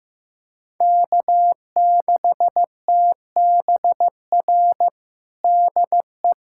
Morse code, heard as a keyed tone, about 16 wpm, 700 Hz. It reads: K6TBR DE
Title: USA Callsign DE Callsign - 5 Characters - 15wpm